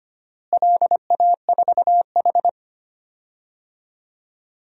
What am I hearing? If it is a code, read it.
Morse code, heard as a keyed tone, about 25 wpm, 700 Hz. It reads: LA4H
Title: DXCC-100 Call Signs - 4 Characters - Repeated 2x - 25wpm